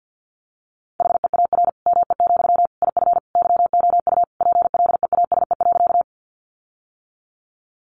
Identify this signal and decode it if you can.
Morse code, heard as a keyed tone, about 50 wpm, 700 Hz. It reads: HEAR ME, IF YOU PLEASE.
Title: Sentences from Top 1500 Words 50wpm